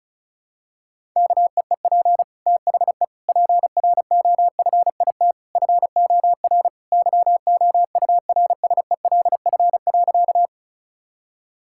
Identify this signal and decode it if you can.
Morse code, heard as a keyed tone, about 34 wpm, 700 Hz. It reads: KEEP THE PROFIT FOR YOURSELF.
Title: Sentences from Top 1000 Words - Review Part 5 - 35wpm